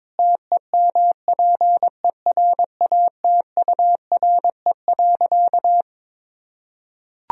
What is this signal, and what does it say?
Morse code, heard as a keyed tone, about 22 wpm, 700 Hz. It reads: TEMPERATURE.
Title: Sentences from Top 1000 Words - Review Part 4 - 22wpm